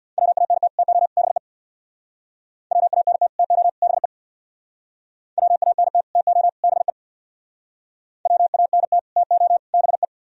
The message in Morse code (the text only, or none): WANT TO BE WANT TO BE WANT TO BE WANT TO BE